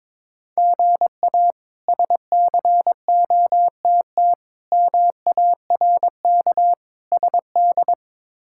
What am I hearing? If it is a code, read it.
Morse code, heard as a keyed tone, about 22 wpm, 700 Hz. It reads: GA SCOTT MARK SD
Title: K1USN SST QSO Practice - 22wpm